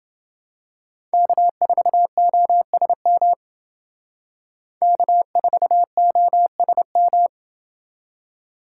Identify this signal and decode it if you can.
Morse code, heard as a keyed tone, about 29 wpm, 700 Hz. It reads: K4OSM K4OSM